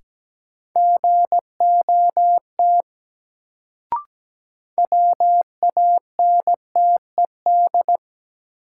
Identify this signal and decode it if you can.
Morse code, heard as a keyed tone, about 17 wpm, 700 Hz. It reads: GOT WANTED
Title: Sets of 2 Words - New Word Drill - 17wpm